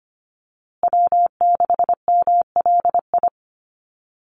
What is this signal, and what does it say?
Morse code, heard as a keyed tone, about 25 wpm, 700 Hz. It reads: W6MLI
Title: USA Call Signs Repeated 2x 25wpm